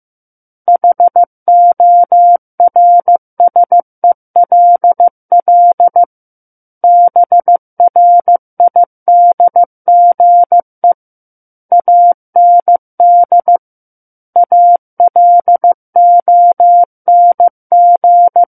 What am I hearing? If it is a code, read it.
Morse code, heard as a keyed tone, about 15 wpm, 700 Hz. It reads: HORSELL BRIDGE AND ALONG
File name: Book One / 06-Chapter 6 - 15 WPM